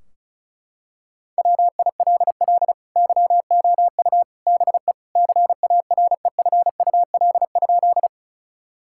Morse code, heard as a keyed tone, about 34 wpm, 700 Hz. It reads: WILL YOU BE CAREFUL?
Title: Sentences from Top 1600 Words 35wpm